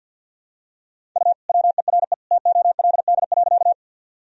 Morse code, heard as a keyed tone, about 51 wpm, 700 Hz. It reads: U WERE TOLD.
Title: Sentences from Top 400 Words and Abbreviations 50wpm